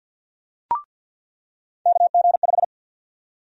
Morse code, heard as a keyed tone, about 50 wpm, 700 Hz. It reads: KG5
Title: USA Call Sign Prefixes - 50wpm